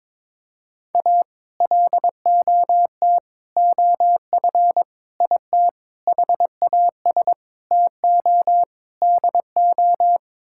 Morse code, heard as a keyed tone, about 22 wpm, 700 Hz. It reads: A LOT OF IT HAS TO DO
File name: practice / 22wpm / Sets of 7 Words - 22wpm